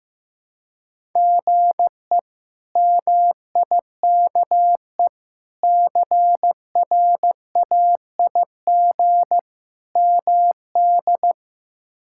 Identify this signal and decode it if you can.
Morse code, heard as a keyed tone, about 15 wpm, 700 Hz. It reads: GE MIKE CRAIG MD
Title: K1USN SST QSO Practice - 15wpm